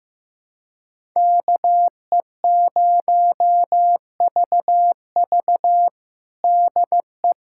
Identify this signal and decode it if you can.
Morse code, heard as a keyed tone, about 15 wpm, 700 Hz. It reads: KE0VV DE
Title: USA Callsign DE Callsign - 5 Characters - 15wpm